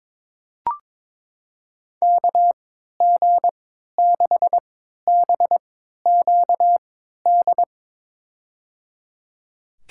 Morse code, heard as a keyed tone, about 22 wpm, spaced as for 13 wpm, 700 Hz. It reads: KG6BQD
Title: USA Call Signs - 6 Characters - 3x Character Spacing - 22wpm